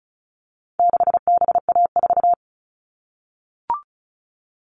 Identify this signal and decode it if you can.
Morse code, heard as a keyed tone, about 34 wpm, 700 Hz. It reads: 6BA4
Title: All Cut Numbers - 4-Digit 35wpm